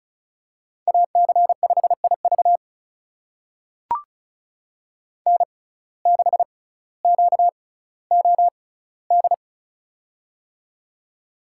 Morse code, heard as a keyed tone, about 36 wpm, spaced as for 18 wpm, 700 Hz. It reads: AC5IV N6QOD
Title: USA Call Signs - 6x Character Spacing - 35wpm